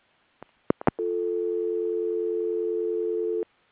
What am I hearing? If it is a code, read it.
Dial tone (North America), 350 + 440 Hz.
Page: S-series, P-series, R-series, and M-series recordings were recorded using a pickup coil.